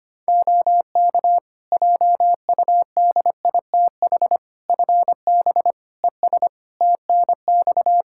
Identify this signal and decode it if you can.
Morse code, heard as a keyed tone, about 25 wpm, 700 Hz. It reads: OK JUDITH FB ES TNX
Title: Practice QSO - Exchange 2 - Info - 25wpm